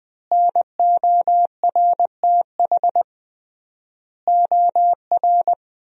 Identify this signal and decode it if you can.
Morse code, heard as a keyed tone, about 20 wpm, 700 Hz. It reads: NORTH OR
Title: 3x Word-Spacing - Sentences from Top 400 Words - 20wpm